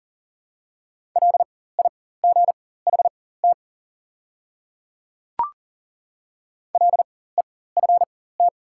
Morse code, heard as a keyed tone, about 40 wpm, spaced as for 19 wpm, 700 Hz. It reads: LIGHT LEFT
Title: 4x Character Spacing - Top 300 Words - 40wpm